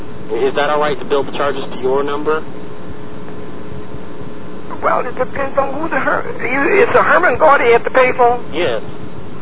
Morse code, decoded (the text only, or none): none